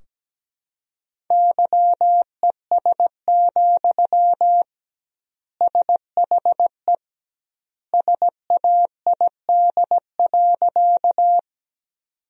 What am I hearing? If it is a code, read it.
Morse code, heard as a keyed tone, about 17 wpm, 700 Hz. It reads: YES, SHE SAID.
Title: 2x Word-Spacing - Sentences from Top 500 Words - 17wpm